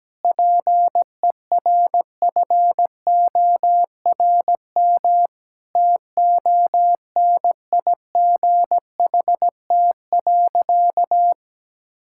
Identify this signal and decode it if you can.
Morse code, heard as a keyed tone, about 17 wpm, 700 Hz. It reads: PERFORM TONIGHT.
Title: Sentences from Top 1000 Words - Review Part 5 - 17wpm